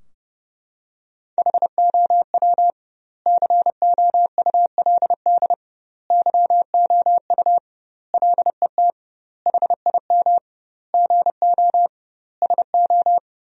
Morse code, heard as a keyed tone, about 31 wpm, 700 Hz. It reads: HOW COULD YOU LET HIM GO SO
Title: 2x Word-Spacing - Sentences from Top 300 Words - 30wpm